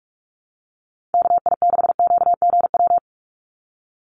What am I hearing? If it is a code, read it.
Morse code, heard as a keyed tone, about 44 wpm, 700 Hz. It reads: KI6QGW
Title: USA Call Signs 45wpm